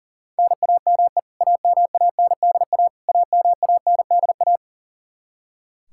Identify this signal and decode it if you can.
Morse code, heard as a keyed tone, about 37 wpm, 700 Hz. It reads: NAME AMANDA AMANDA
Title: K1USN SST QSO Practice 40wpm